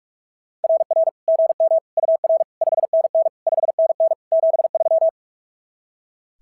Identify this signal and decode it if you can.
Morse code, heard as a keyed tone, about 44 wpm, 690 Hz. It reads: RR GM UR 5NN 5NN 73
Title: SOTA QSO Practice 45wpm